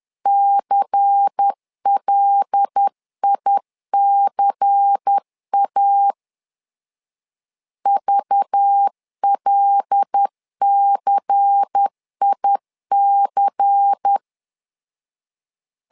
Morse code, heard as a keyed tone, about 10 wpm, 800 Hz. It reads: CLICA VLCIC
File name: LEZIONI / 5 LBVC / 5 LBVC 10wpm 2349